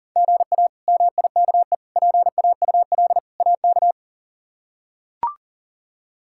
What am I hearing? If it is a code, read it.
Morse code, heard as a keyed tone, about 40 wpm, 700 Hz. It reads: GA MIKE PAUL AK
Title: K1USN SST QSO Practice - 40wpm